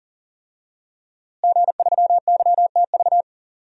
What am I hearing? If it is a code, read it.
Morse code, heard as a keyed tone, about 40 wpm, 700 Hz. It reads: G3YTV
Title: DXCC-100 Call Signs - 5 Characters - Repeated 3x - 40wpm